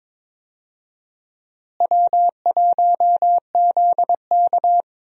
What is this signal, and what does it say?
Morse code, heard as a keyed tone, about 22 wpm, 700 Hz. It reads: W1ZK
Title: USA Call Signs - 4 Characters - Repeated 3x - 22wpm